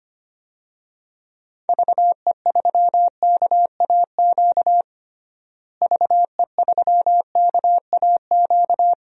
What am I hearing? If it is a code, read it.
Morse code, heard as a keyed tone, about 25 wpm, 700 Hz. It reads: VE3KAQ VE3KAQ
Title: DXCC-100 Call Signs - 6 Characters - Repeated 2x - 25wpm